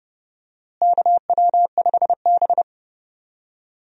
Morse code, heard as a keyed tone, about 30 wpm, 700 Hz. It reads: KW5B
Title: USA Call Signs - 4 Characters - Repeated 3x - 30wpm